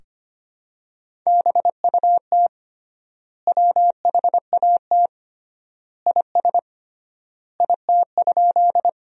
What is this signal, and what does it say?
Morse code, heard as a keyed tone, about 25 wpm, 700 Hz. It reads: BUT WHAT IS IT?
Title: 3x Word-Spacing - Sentences from Top 100 Words - 25wpm